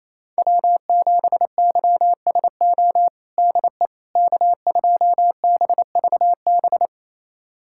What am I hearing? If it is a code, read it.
Morse code, heard as a keyed tone, about 28 wpm, 700 Hz. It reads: W7YSO DE K2BVB
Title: USA Callsign DE Callsign - 5 Characters - 28wpm